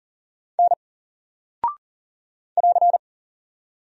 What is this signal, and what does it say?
Morse code, heard as a keyed tone, about 40 wpm, 700 Hz. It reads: N <AR>
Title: Single Character (Letter:Number:Punctuation:Prosign) Alt 40wpm